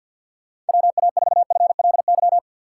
Morse code, heard as a keyed tone, about 52 wpm, 700 Hz. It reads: WA4FLY
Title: USA Call Signs - 6 Characters - 4x Character Spacing - 50wpm